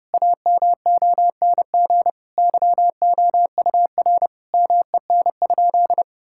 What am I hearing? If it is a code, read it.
Morse code, heard as a keyed tone, about 32 wpm, 700 Hz. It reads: AMONG YOUR MEN?
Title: Sentences from Top 1000 Words - Review Part 6 - 30wpm